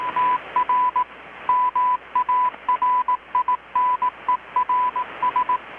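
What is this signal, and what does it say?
Morse code, heard as a keyed tone, about 18 wpm, 1000 Hz, with radio noise. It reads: TR MARINERS
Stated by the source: Morse from KPH's Night of Nights XX maritime radiotelegraphy event, received in Chiapas, Mexico about 3,800 km away on a Kenwood R-600 with a grounded 10 m long-wire antenna.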